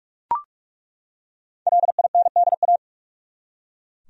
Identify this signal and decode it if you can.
Morse code, heard as a keyed tone, about 41 wpm, 700 Hz. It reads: LINDA
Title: USA Common Names - 45wpm